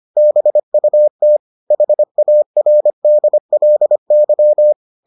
Morse code, heard as a keyed tone, about 25 wpm, 600 Hz. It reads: BUT HARDLY